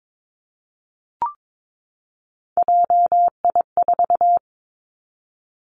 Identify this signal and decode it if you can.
Morse code, heard as a keyed tone, about 22 wpm, 700 Hz. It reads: JI4
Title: Three Challenging Letter-Number Combinations 22wpm